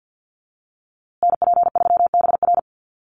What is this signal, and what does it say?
Morse code, heard as a keyed tone, about 50 wpm, 700 Hz. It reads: NP3BR
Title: USA Call Signs Repeated 2x 50wpm